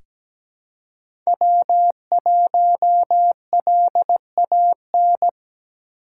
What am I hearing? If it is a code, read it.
Morse code, heard as a keyed tone, about 17 wpm, 700 Hz. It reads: W1LAN
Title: USA Call Signs - 5 Characters - 4x Character Spacing - 17wpm